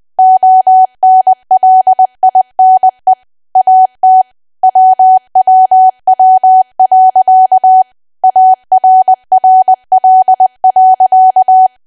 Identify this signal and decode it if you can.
Morse code, heard as a keyed tone, about 20 wpm, 750 Hz. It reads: ONLINE AT WWW.ARRL.
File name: cwa / W1AW-Level3 / 20WPM